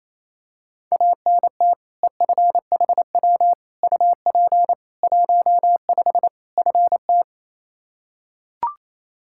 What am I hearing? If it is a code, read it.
Morse code, heard as a keyed tone, about 28 wpm, 700 Hz. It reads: ANT EFHW UP 15 FT
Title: Practice QSO - Exchange 2 - Info - 28wpm